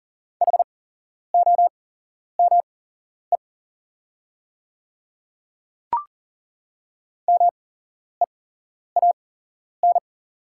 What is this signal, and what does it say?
Morse code, heard as a keyed tone, about 37 wpm, spaced as for 11 wpm, 690 Hz. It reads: HOME MEAN
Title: 8x Character Spacing - Top 500 Words - 40wpm